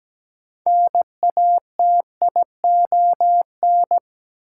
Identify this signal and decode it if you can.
Morse code, heard as a keyed tone, about 17 wpm, 700 Hz. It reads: NATION
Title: Sentences from Top 900 Words - New Word Drill - 17wpm